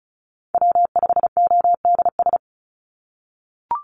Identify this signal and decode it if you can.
Morse code, heard as a keyed tone, about 36 wpm, 700 Hz. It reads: W5ODS
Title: NA SOTA Chaser Callsigns 35wpm